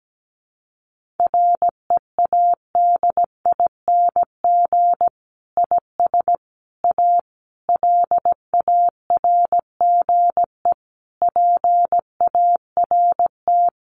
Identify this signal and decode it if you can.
Morse code, heard as a keyed tone, about 17 wpm, 700 Hz. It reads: READING IS A LARGE PART
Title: Sentences from Top 1400 Words 17wpm